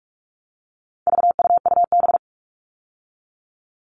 Morse code, heard as a keyed tone, about 44 wpm, 700 Hz. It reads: VUUB
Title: All Cut Numbers - 4-Digit 45wpm